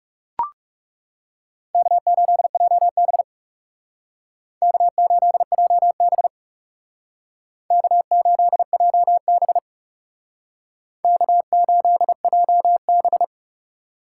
Morse code, heard as a keyed tone, about 38 wpm, 700 Hz. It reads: K8JB K8JB K8JB K8JB